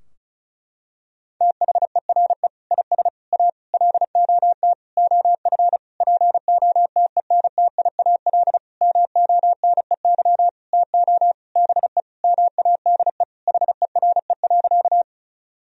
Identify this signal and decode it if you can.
Morse code, heard as a keyed tone, about 36 wpm, 700 Hz. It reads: THERE IS A LOT OF POTENTIAL MONEY TO BE MADE HERE.
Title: Sentences from Top 1000 Words - Review Part 4 - 35wpm